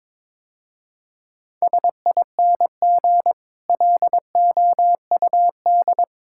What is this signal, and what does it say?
Morse code, heard as a keyed tone, about 22 wpm, 700 Hz. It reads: SING LOUD
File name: practice / 22wpm / Easy Sentences with Hints - 22wpm